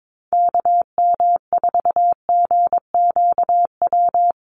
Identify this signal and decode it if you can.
Morse code, heard as a keyed tone, about 22 wpm, 700 Hz. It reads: KM4GQW
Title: USA Call Signs 22wpm